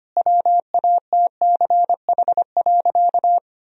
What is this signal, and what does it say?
Morse code, heard as a keyed tone, about 25 wpm, 700 Hz. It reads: WATCH.
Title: Sentences from Top 1000 Words - Review Part 1 - 25wpm